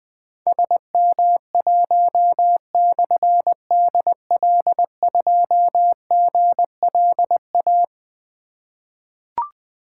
Morse code, heard as a keyed tone, about 20 wpm, 700 Hz. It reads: SM1/DL2GLA
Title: Difficult Call Signs 20wpm